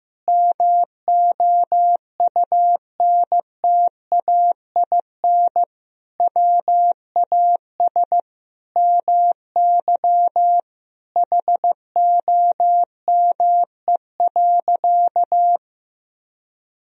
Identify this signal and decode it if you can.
Morse code, heard as a keyed tone, about 15 wpm, 700 Hz. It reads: MOUNTAIN WAS MY HOME.
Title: Sentences from Top 1000 Words - Review Part 1 - 15wpm